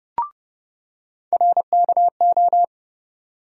Morse code, heard as a keyed tone, about 30 wpm, 700 Hz. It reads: RKO
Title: USA Call Sign Postfixes - 3 Letters 30wpm